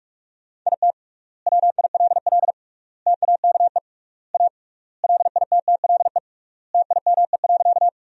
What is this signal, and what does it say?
Morse code, heard as a keyed tone, about 41 wpm, 700 Hz. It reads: IT WILL TAKE A LITTLE TIME.
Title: 3x Word-Spacing - Sentences from Top 200 Words - 45wpm